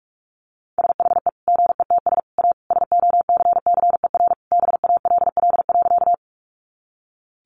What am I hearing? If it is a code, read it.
Morse code, heard as a keyed tone, about 44 wpm, 700 Hz. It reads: SHE GETS A SOCCER BALL.
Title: Easy Phrases 45wpm